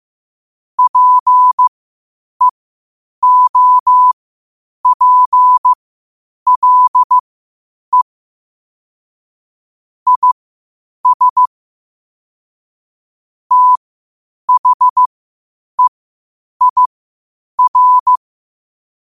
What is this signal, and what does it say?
Morse code, heard as a keyed tone, about 15 wpm, spaced as for 8 wpm, 1000 Hz. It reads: PEOPLE IS THEIR
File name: qotdfiles / 8wpm